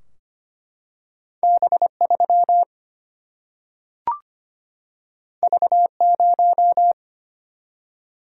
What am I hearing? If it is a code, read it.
Morse code, heard as a keyed tone, about 25 wpm, 700 Hz. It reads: B3 V0
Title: Single-Letter Single-Number 25wpm